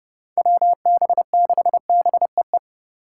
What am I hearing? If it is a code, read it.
Morse code, heard as a keyed tone, about 32 wpm, 700 Hz. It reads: WB6BEE
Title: K1USN SST QSO Practice - 30wpm